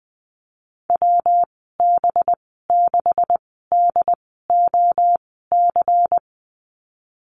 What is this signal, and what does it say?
Morse code, heard as a keyed tone, about 20 wpm, spaced as for 15 wpm, 700 Hz. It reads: WB6DOC
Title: 200 Lesson 32g - Letter k - Call Signs 20wpm